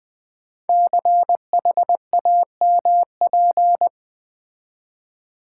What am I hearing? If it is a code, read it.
Morse code, heard as a keyed tone, about 20 wpm, 700 Hz. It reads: CHAMP
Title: Build A Word - 5 Letters At A Time - 12-Letter Words - 20wpm